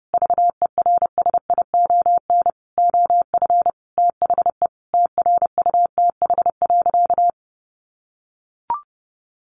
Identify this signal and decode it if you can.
Morse code, heard as a keyed tone, about 30 wpm, 700 Hz. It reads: VERSION OF THE TRUTH.
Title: Sentences from Top 1400 Words 30wpm